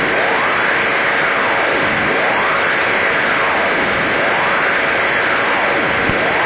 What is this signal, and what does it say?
Emergency Alert System (SAME) tones (header bursts), heard as harsh buzzing bursts. It is heard from 0.6 to 5.3 s.
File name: uploader / 10115 usb